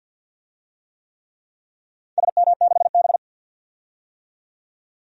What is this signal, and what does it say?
Morse code, heard as a keyed tone, about 52 wpm, 700 Hz. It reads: SM6B
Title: DXCC-100 Call Signs - 4 Characters - 50wpm